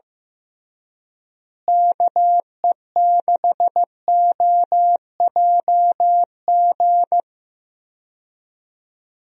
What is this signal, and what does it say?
Morse code, heard as a keyed tone, about 15 wpm, 700 Hz. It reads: KE6OJG
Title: USA Call Signs - 6 Characters - Repeated 3x - 15wpm